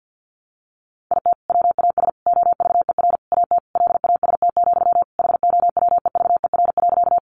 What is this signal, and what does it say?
Morse code, heard as a keyed tone, about 50 wpm, 700 Hz. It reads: IT WAS OVER AT LAST, HOWEVER.
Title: Sentences from Top 600 Words 50wpm